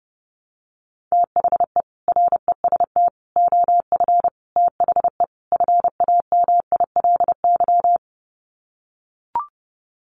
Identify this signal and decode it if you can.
Morse code, heard as a keyed tone, about 30 wpm, 700 Hz. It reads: THE REST OF THE FAMILY
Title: Sets of 5 Words Encore2 30wpm